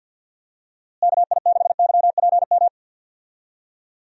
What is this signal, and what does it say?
Morse code, heard as a keyed tone, about 47 wpm, 700 Hz. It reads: KI6YPM
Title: USA Call Signs - 6 Characters - Repeated 3x - 50wpm